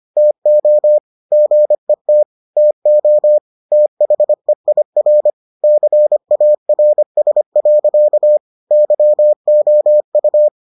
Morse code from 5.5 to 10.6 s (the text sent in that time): CARS. YOU